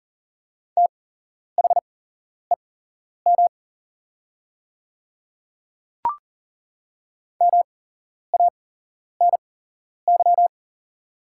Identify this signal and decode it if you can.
Morse code, heard as a keyed tone, about 40 wpm, spaced as for 11 wpm, 700 Hz. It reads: THEM MANY
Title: 8x Character Spacing - Top 100 Words - 40wpm